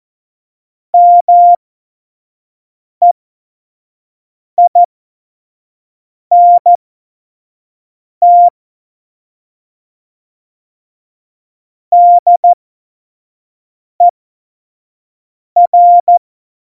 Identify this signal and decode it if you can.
Morse code, heard as a keyed tone, about 14 wpm, spaced as for 5 wpm, 700 Hz. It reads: MEINT DER
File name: Morsekurs / Witze / Witz 26. September 17, Tempo 25